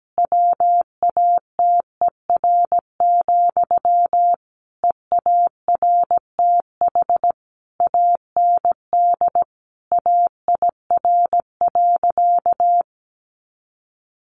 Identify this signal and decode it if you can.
Morse code, heard as a keyed tone, about 17 wpm, 700 Hz. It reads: WATER, EARTH AND AIR.
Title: Sentences from Top 400 Words 17wpm